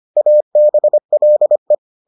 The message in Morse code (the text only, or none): ABLE